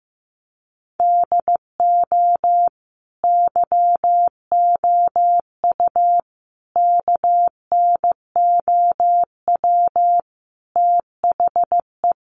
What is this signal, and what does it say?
Morse code, heard as a keyed tone, about 14 wpm, 700 Hz. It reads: DO YOU KNOW THE
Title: Sentences from Top 1000 Words 15wpm